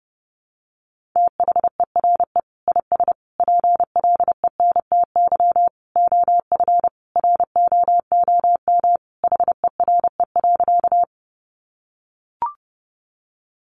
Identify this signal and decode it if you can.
Morse code, heard as a keyed tone, about 30 wpm, 700 Hz. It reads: THERE IS PLENTY OF ROOM HERE.
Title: Sentences from Top 1300 Words 30wpm